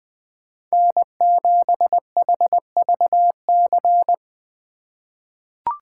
Morse code, heard as a keyed tone, about 20 wpm, 700 Hz. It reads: N7HVC
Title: USA Call Signs - 5 Characters - 4x Character Spacing - 20wpm